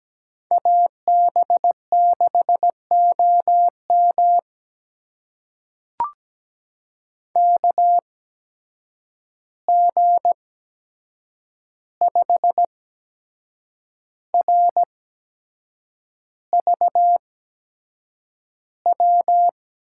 Morse code, heard as a keyed tone, about 17 wpm, spaced as for 7 wpm, 700 Hz. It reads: AB6OM KG5RVW
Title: USA Call Signs - 8x Character Spacing - 17wpm